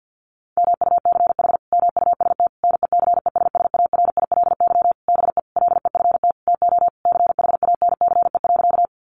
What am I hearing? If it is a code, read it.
Morse code, heard as a keyed tone, about 50 wpm, 700 Hz. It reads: MUCH MUST NECESSARILY BE LEFT TO CHANCE.
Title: Sentences from Top 1400 Words 50wpm